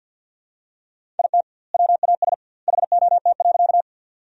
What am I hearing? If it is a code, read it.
Morse code, heard as a keyed tone, about 52 wpm, 700 Hz. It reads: IT WAS HOT.
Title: 2x Word-Spacing - Sentences from Top 500 Words - 50wpm